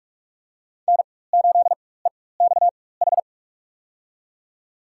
Morse code, heard as a keyed tone, about 46 wpm, spaced as for 21 wpm, 700 Hz. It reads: N8EXH
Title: USA Call Signs - 5 Characters - 4x Character Spacing - 45wpm